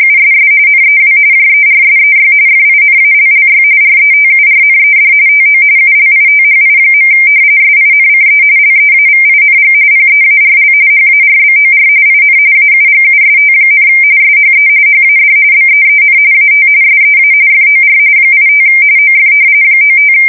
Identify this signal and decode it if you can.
RTTY, about 45.45 baud. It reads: USS NEW JERSEY FINALLY RETURNED TO ULITHI. SINCE LEAVING PEARL HARBOR IN AUGUST, SHE HAD BEEN AT SEA FOR 85 OF 95
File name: fsk170 / battleshipnj